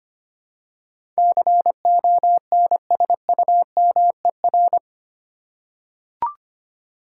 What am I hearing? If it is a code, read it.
Morse code, heard as a keyed tone, about 25 wpm, 700 Hz. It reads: CONSUMER
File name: Eight Letter Words - 25wpm